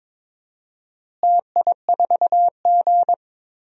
Morse code, heard as a keyed tone, about 21 wpm, 700 Hz. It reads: TI4G
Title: DXCC-100 Call Signs - 4 Characters - Repeated 2x - 22wpm